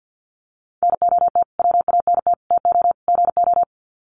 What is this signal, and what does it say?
Morse code, heard as a keyed tone, about 50 wpm, 700 Hz. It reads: NOT WANT TO GO
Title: Sets of 4 Words 50wpm